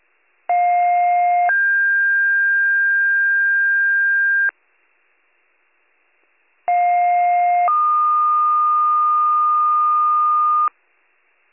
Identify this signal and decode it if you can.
Two-tone page, heard as a one-second tone, then a three-second tone, about 710 then 1600 Hz, about 710 then 1220 Hz. It is heard from 0.5 to 10.7 s.